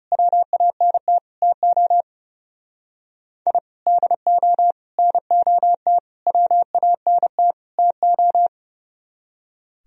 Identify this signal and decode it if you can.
Morse code, heard as a keyed tone, about 34 wpm, 700 Hz. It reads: WANT TO I DO NOT WANT TO
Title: Speed-Racing - Sets of 5 Words 30wpm